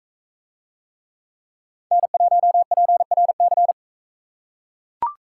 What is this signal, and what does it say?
Morse code, heard as a keyed tone, about 42 wpm, 700 Hz. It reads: N1PRC